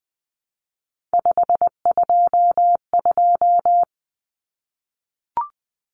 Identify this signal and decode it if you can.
Morse code, heard as a keyed tone, about 20 wpm, 700 Hz. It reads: 522
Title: Common Cut Numbers - 3-Digit 20wpm